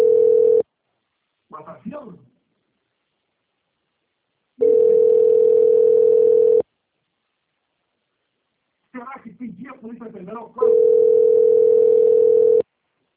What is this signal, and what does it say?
Ringback tone (North America), 440 + 480 Hz, with speech.
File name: out- - - - - .25216